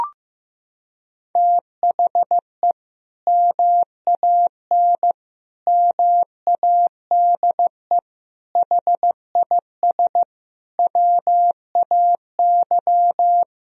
Morse code, heard as a keyed tone, about 14 wpm, 700 Hz. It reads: THE MAN MADE HIS WAY
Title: Sentences from Top 1000 Words - Review Part 2 - 15wpm